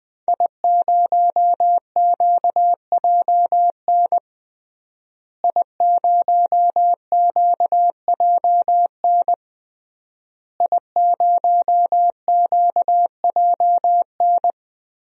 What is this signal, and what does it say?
Morse code, heard as a keyed tone, about 20 wpm, 700 Hz. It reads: I0QJN I0QJN I0QJN
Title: DXCC-100 Call Signs - 5 Characters - Repeated 3x - 20wpm